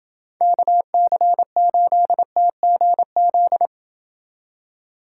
Morse code, heard as a keyed tone, about 28 wpm, 700 Hz. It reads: KC8TGZ